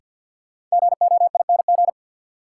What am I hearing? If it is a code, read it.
Morse code, heard as a keyed tone, about 52 wpm, 700 Hz. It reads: GOING
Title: Sets of 2 Words - New Word Drill - 50wpm